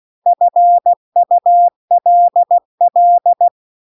Morse code, heard as a keyed tone, about 16 wpm, 700 Hz. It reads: FULL